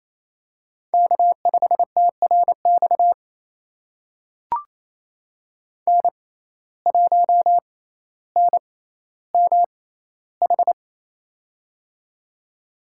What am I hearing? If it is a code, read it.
Morse code, heard as a keyed tone, about 28 wpm, spaced as for 14 wpm, 700 Hz. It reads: K5TRX N1NMH
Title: USA Call Signs - 5 Characters - 6x Character Spacing - 28wpm